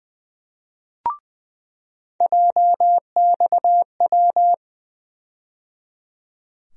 Morse code, heard as a keyed tone, about 20 wpm, 700 Hz. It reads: JXW
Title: USA Call Sign Postfixes - 3 Letters 20wpm